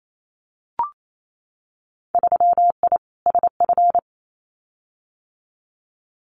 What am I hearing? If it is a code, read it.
Morse code, heard as a keyed tone, about 28 wpm, 700 Hz. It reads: 3I SF
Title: Winter Field Day Exchanges 28wpm